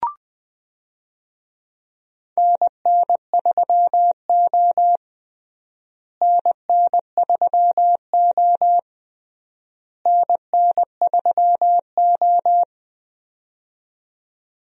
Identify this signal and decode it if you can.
Morse code, heard as a keyed tone, about 20 wpm, 700 Hz. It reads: NN3O NN3O NN3O
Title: USA Call Signs - 4 Characters - Repeated 3x - 20wpm